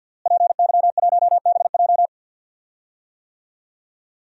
Morse code, heard as a keyed tone, about 52 wpm, 700 Hz. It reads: PY1BJ
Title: DXCC-100 Call Signs - 5 Characters - Repeated 3x - 50wpm